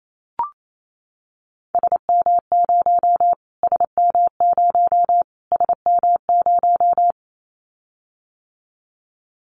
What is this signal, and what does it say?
Morse code, heard as a keyed tone, about 28 wpm, 700 Hz. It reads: SM0 SM0 SM0
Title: Callsign Prefixes for Top 25 DXCC - Repeated 3x - 28wpm